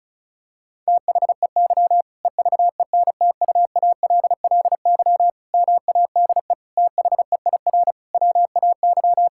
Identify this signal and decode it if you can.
Morse code, heard as a keyed tone, about 34 wpm, 700 Hz. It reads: THEY EVENTUALLY MADE THEIR WAY
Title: Sentences from Top 1000 Words - Review Part 6 - 35wpm